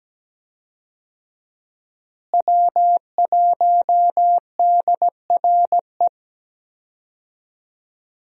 Morse code, heard as a keyed tone, about 17 wpm, 700 Hz. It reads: W1DRE
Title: USA Call Signs - 5 Characters - Repeated 2x - 17wpm